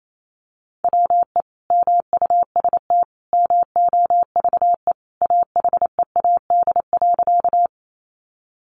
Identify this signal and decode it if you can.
Morse code, heard as a keyed tone, about 28 wpm, 700 Hz. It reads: WE MUST MOVE AHEAD.
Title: Sentences from Top 1100 Words 28wpm